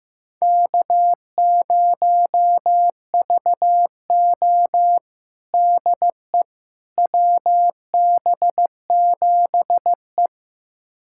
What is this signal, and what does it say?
Morse code, heard as a keyed tone, about 16 wpm, 700 Hz. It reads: K0VO DE WB7E
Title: USA Callsign DE Callsign - 4 Characters - 15wpm